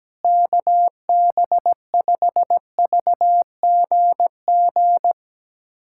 Morse code, heard as a keyed tone, about 17 wpm, 700 Hz. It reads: KB5VGG
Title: USA Callsign DE Callsign - 6 Characters - 17wpm